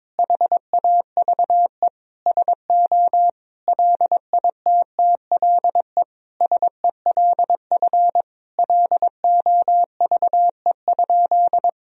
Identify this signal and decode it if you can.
Morse code, heard as a keyed tone, about 21 wpm, 700 Hz. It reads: HAVE SO LITTLE SELF LOVE?
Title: Sentences from Top 1000 Words - Review Part 1 - 22wpm